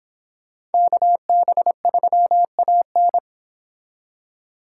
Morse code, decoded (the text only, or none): KB3AN